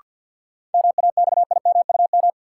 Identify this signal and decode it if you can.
Morse code, heard as a keyed tone, about 48 wpm, 700 Hz. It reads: MAXIMUM
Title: Sentences from Top 1300 Words - New Word Drill - 50wpm